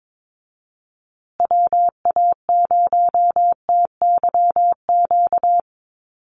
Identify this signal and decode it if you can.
Morse code, heard as a keyed tone, about 22 wpm, 700 Hz. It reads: WA0TYQ
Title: USA Call Signs Repeated 2x 22wpm